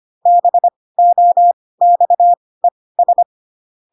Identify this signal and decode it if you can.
Morse code, heard as a keyed tone, about 25 wpm, spaced as for 18 wpm, 700 Hz. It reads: BOXES
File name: Pinocchio / Pinocchio ch5 12 25 ik1hge